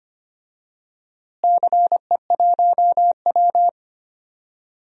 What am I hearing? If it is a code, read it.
Morse code, heard as a keyed tone, about 25 wpm, 700 Hz. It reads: CE1W
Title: DXCC-100 Call Signs - 4 Characters - Repeated 2x - 25wpm